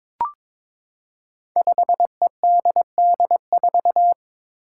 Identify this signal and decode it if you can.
Morse code, heard as a keyed tone, about 22 wpm, 700 Hz. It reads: 5EDD4
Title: Five Challenging Letter-Number Combinations 22wpm